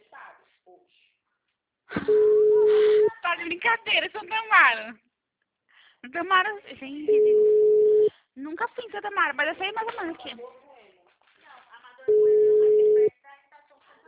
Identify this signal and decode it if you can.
Ringback tone (Europe, 425 Hz), with speech.
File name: out- -2002- - - .43682